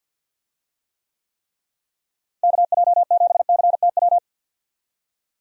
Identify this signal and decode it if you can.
Morse code, heard as a keyed tone, about 51 wpm, 700 Hz. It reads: KJ7CTW
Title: USA Call Signs - 6 Characters - Repeated 3x - 50wpm